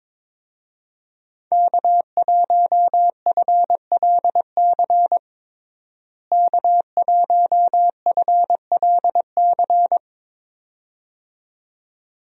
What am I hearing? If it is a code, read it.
Morse code, heard as a keyed tone, about 22 wpm, 700 Hz. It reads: K1FLC K1FLC
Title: USA Call Signs - 5 Characters - Repeated 2x - 22wpm